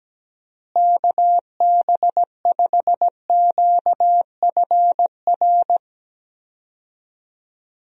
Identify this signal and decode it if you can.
Morse code, heard as a keyed tone, about 17 wpm, 700 Hz. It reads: KB5QFR
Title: USA Call Signs - 6 Characters - Repeated 3x - 17wpm